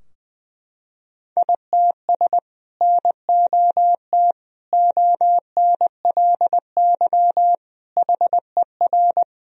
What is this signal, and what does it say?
Morse code, heard as a keyed tone, about 20 wpm, 700 Hz. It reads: ITS NOT ONLY HER
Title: Sentences from Top 1000 Words - Review Part 2 - 20wpm